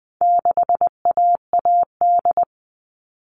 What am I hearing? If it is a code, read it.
Morse code, heard as a keyed tone, about 20 wpm, 700 Hz. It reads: 6AAD
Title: All Cut Numbers - 4-Digit 20wpm